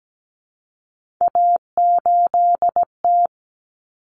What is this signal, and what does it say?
Morse code, heard as a keyed tone, about 17 wpm, 700 Hz. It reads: A8T
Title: Common Cut Numbers - 3-Digit 17wpm